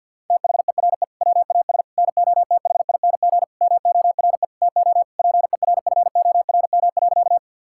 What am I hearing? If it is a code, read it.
Morse code, heard as a keyed tone, about 46 wpm, 690 Hz. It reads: THERE WAS NOTHING MORE TO PERFORM.
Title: Sentences from Top 1000 Words - Review Part 5 - 50wpm